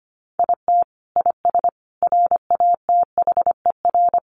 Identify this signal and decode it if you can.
Morse code, heard as a keyed tone, about 25 wpm, 700 Hz. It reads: IT IS RATHER
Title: Sentences from Top 700 Words 25wpm